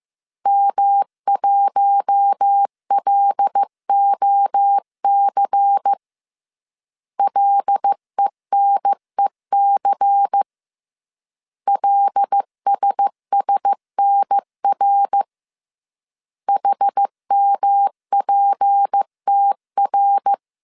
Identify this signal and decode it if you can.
Morse code, heard as a keyed tone, about 15 wpm, 800 Hz. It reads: M1LOC LENEC LSSNR HMPTR
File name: CW / DET / 1 6 1500C S C 14wpm 2951